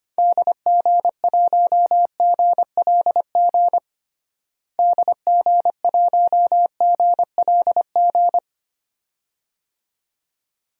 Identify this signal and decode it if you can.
Morse code, heard as a keyed tone, about 25 wpm, 700 Hz. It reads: DG1GLG DG1GLG
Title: DXCC-100 Call Signs - 6 Characters - Repeated 2x - 25wpm